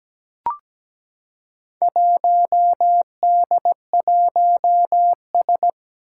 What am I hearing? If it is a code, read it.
Morse code, heard as a keyed tone, about 17 wpm, 700 Hz. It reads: 1D1S
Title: Four Challenging Letter-Number Combinations 17wpm